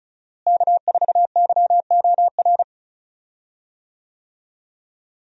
Morse code, heard as a keyed tone, about 34 wpm, 700 Hz. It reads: K4YOR
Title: USA Call Signs - 5 Characters - Repeated 2x - 35wpm